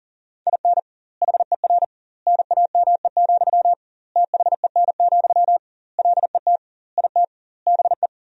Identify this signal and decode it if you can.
Morse code, heard as a keyed tone, about 40 wpm, 700 Hz. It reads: IN HER NAME, THEN, LET IT BE
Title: 2x Word-Spacing - Sentences from Top 500 Words - 40wpm